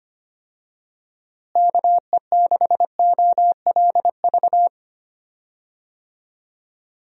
Morse code, heard as a keyed tone, about 25 wpm, 700 Hz. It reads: KE6OLV
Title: USA Call Signs - 6 Characters - Repeated 3x - 25wpm